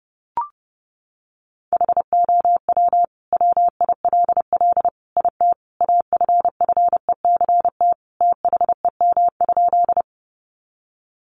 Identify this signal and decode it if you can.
Morse code, heard as a keyed tone, about 30 wpm, 700 Hz. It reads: HOW WILL IT AFFECT THEM?
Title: Sentences from Top 900 Words 30wpm